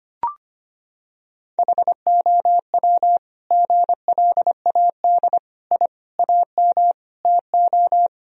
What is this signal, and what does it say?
Morse code, heard as a keyed tone, about 25 wpm, 700 Hz. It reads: HOW GLAD I AM TO
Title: Sentences from Top 1000 Words - Review Part 2 - 25wpm